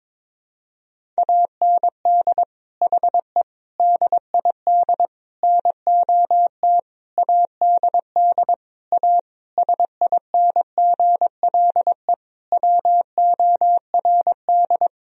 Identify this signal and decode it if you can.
Morse code, heard as a keyed tone, about 22 wpm, 700 Hz. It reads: AND HE DID NOT ADD A SINGLE WORD
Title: Sentences from Top 1000 Words - Review Part 1 - 22wpm